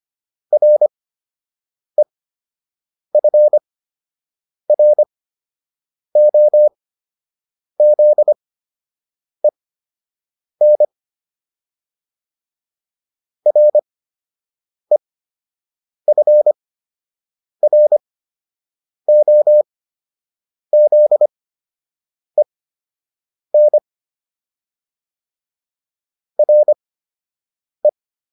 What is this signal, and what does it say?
Morse code, heard as a keyed tone, about 25 wpm, spaced as for 7 wpm, 600 Hz. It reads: REFROZEN REFROZEN RE
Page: fundamental-practice-files